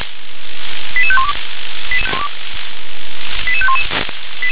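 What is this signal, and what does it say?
Five-tone selective call (ZVEI), heard as quick runs of five tones, tones of about 80 ms. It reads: 88512 88511
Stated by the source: ZVEI five-tone call 88510 received by SDR.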